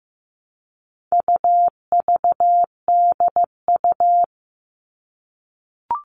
Morse code, heard as a keyed tone, about 15 wpm, 700 Hz. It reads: UVDU
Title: All Cut Numbers - 4-Digit 15wpm